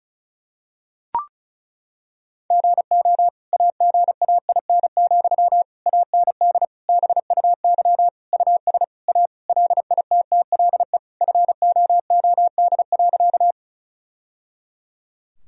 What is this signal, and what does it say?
Morse code, heard as a keyed tone, about 34 wpm, 700 Hz. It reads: GO AGAIN, AND BUY US A LITTLE FOOD.
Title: Sentences from Top 1000 Words - Review Part 4 - 35wpm